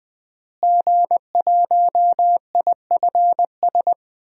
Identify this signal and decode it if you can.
Morse code, heard as a keyed tone, about 19 wpm, 700 Hz. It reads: G1IFS
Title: DXCC-100 Call Signs - 5 Characters - Repeated 3x - 20wpm